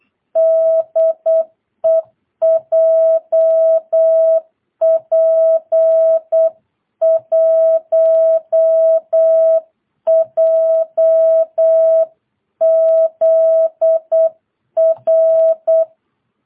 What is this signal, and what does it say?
Morse code, heard as a keyed tone, about 8 wpm, 650 Hz. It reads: DE JP1JZR
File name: jh8cbh / 2022cw